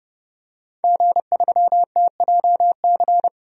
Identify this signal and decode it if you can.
Morse code, heard as a keyed tone, about 30 wpm, 700 Hz. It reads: G3TJC
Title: DXCC-100 Call Signs - 5 Characters - 30wpm